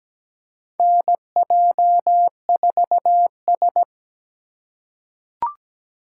Morse code, heard as a keyed tone, about 17 wpm, 700 Hz. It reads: NJ4S
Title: USA Call Signs - 3x Character Spacing - 17wpm